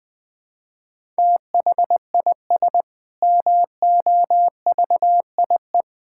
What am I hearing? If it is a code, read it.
Morse code, heard as a keyed tone, about 19 wpm, 700 Hz. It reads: THIS MOVIE
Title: Sentences with 6-Word Phrases - Spoken Beginning - 20wpm